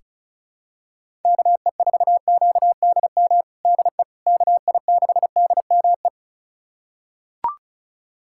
Morse code, heard as a keyed tone, about 34 wpm, 700 Hz. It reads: KE4QDM DE KI6DME
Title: USA Callsign DE Callsign - 6 Characters - 35wpm